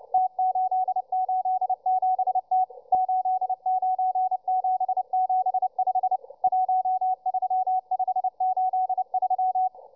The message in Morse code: T887T 89775 13583